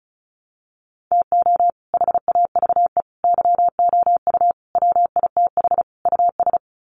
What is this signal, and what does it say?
Morse code, heard as a keyed tone, about 34 wpm, 700 Hz. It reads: TO HAVE YOU WITH US
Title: Sets of 5 Words 35wpm